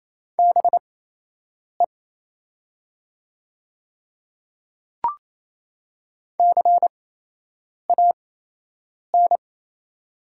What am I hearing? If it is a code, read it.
Morse code, heard as a keyed tone, about 27 wpm, spaced as for 8 wpm, 700 Hz. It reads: BE CAN
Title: 8x Character Spacing - Top 100 Words - 28wpm